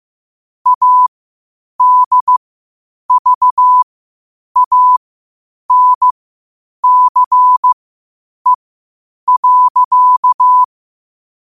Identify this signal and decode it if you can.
Morse code, heard as a keyed tone, about 15 wpm, spaced as for 8 wpm, 1000 Hz. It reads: ADVANCE.
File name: qotdfiles / 8wpm